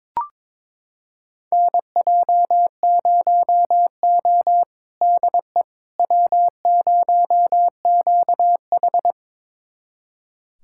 Morse code, heard as a keyed tone, about 22 wpm, 700 Hz. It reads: NJ0O DE W0QH
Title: USA Callsign DE Callsign - 4 Characters - 22wpm